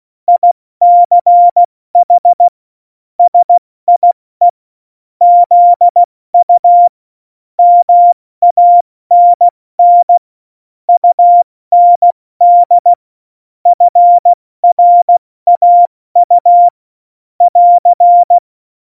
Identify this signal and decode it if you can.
Morse code, heard as a keyed tone, about 16 wpm, spaced as for 14 wpm, 700 Hz. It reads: ICH SIE ZU MANN UND FRAU <AR>
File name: Morsekurs / Witze / Witz 30. Juli 2019, 70 F80